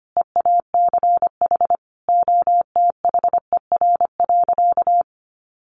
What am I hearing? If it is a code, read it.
Morse code, heard as a keyed tone, about 25 wpm, 700 Hz. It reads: EACH OTHER.
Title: Easy Phrases 25wpm